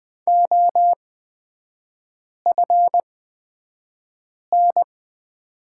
Morse code, heard as a keyed tone, about 20 wpm, spaced as for 5 wpm, 700 Hz. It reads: OFN
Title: 224 Lesson 36e - BT - Three Characters 20wpm